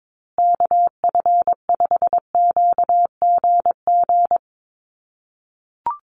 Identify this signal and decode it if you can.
Morse code, heard as a keyed tone, about 22 wpm, 700 Hz. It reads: KF5QGG
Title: USA Call Signs Repeated 2x 22wpm